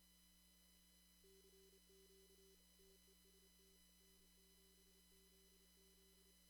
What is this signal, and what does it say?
Morse code, heard as a keyed tone, about 28 wpm, 340 Hz. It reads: CQ TEST N7ZZ
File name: CW / kt5j wpx cw-op-k5pi